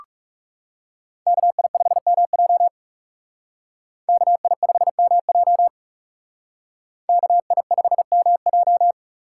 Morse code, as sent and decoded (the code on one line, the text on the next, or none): -.- .. ..... -- .--- / -.- .. ..... -- .--- / -.- .. ..... -- .---
KI5MJ KI5MJ KI5MJ